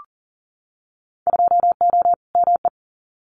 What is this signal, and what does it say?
Morse code, heard as a keyed tone, about 39 wpm, 700 Hz. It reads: 2O ME
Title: Winter Field Day Exchanges Encore 40wpm